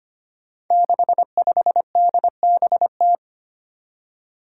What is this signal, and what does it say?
Morse code, heard as a keyed tone, about 25 wpm, 700 Hz. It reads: 65DBT
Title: Five Challenging Letter-Number Combinations 25wpm